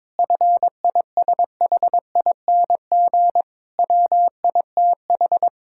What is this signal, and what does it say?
Morse code, heard as a keyed tone, about 22 wpm, 700 Hz. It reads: FISHING WITH
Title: Sentences from Top 1000 Words - Review Part 5 - 22wpm